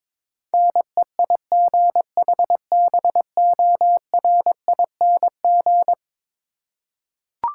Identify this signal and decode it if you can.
Morse code, heard as a keyed tone, about 22 wpm, 700 Hz. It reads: NEIGHBORING
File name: practice / 22wpm / Eleven Letter Words - 22wpm